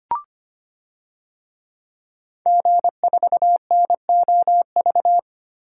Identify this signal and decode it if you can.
Morse code, heard as a keyed tone, about 25 wpm, 700 Hz. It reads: G4NOV
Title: DXCC-100 Call Signs - 5 Characters - Repeated 2x - 25wpm